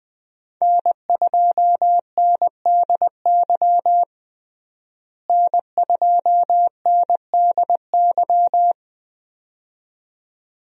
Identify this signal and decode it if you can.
Morse code, heard as a keyed tone, about 20 wpm, 700 Hz. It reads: N2NDY N2NDY
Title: USA Call Signs - 5 Characters - Repeated 2x - 20wpm